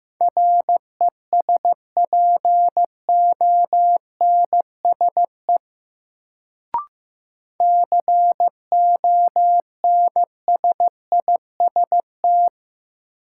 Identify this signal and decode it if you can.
Morse code, heard as a keyed tone, about 15 wpm, 700 Hz. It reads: RESPONSE CONSIST
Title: Sentences from Top 1400 Words - New Word Drill - 15wpm